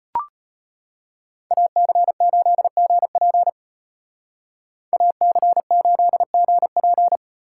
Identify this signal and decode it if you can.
Morse code, heard as a keyed tone, about 35 wpm, 700 Hz. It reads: AC8GP AC8GP